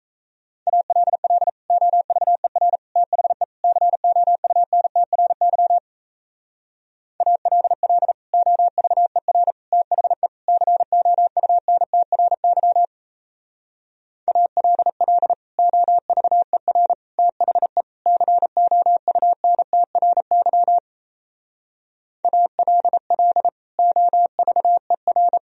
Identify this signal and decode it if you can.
Morse code, heard as a keyed tone, about 37 wpm, 700 Hz. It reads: ALL OVER THE COUNTRY ALL OVER THE COUNTRY ALL OVER THE COUNTRY ALL OVER